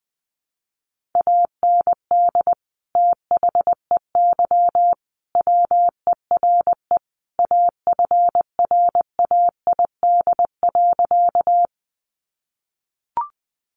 Morse code, heard as a keyed tone, about 20 wpm, 700 Hz. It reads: AND THEY WERE AFRAID.
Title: Sentences from Top 1500 Words 20wpm